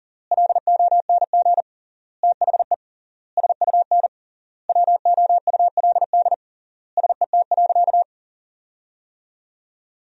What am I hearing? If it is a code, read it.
Morse code, heard as a keyed tone, about 40 wpm, 700 Hz. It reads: LONG THE SUN WOULD SET.
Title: 3x Word-Spacing - Sentences from Top 400 Words - 40wpm